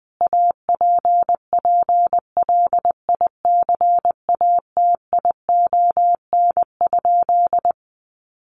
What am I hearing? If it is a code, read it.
Morse code, heard as a keyed tone, about 20 wpm, 700 Hz. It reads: APPLICATION?
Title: Sentences from Top 1200 Words 20wpm